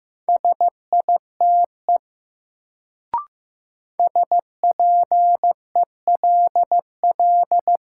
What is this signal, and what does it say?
Morse code, heard as a keyed tone, about 15 wpm, 700 Hz. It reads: SITE SPELL
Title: Sentences from Top 1000 Words - Review Part 1 - New Word Drill - 15wpm